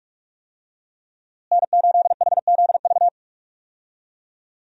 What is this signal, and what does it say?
Morse code, heard as a keyed tone, about 41 wpm, 700 Hz. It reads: N8HZV
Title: USA Call Signs - 5 Characters - Repeated 3x - 45wpm